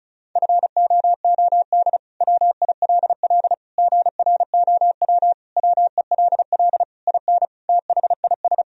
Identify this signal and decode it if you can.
Morse code, heard as a keyed tone, about 36 wpm, 700 Hz. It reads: FOOD WILL GROW WELL IN THIS
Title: Sentences from Top 1000 Words - Review Part 2 - 35wpm